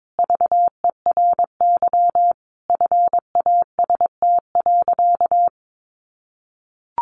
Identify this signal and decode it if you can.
Morse code, heard as a keyed tone, about 22 wpm, 700 Hz. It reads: VERY FAST.
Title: Sentences from Top 1100 Words 22wpm